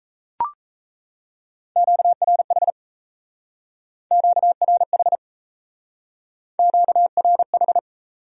Morse code, heard as a keyed tone, about 34 wpm, 700 Hz. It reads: QRH QRH QRH
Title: Speed-Racing Q-Codes - 28wpm